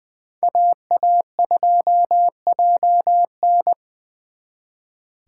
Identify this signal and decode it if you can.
Morse code, heard as a keyed tone, about 24 wpm, 700 Hz. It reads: AA2JN
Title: Speed Racing - USA Call Signs 20wpm